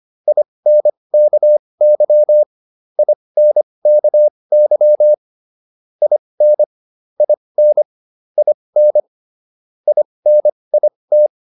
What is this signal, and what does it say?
Morse code, heard as a keyed tone, about 25 wpm, spaced as for 20 wpm, 600 Hz. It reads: INKY INKY IN IN IN INIT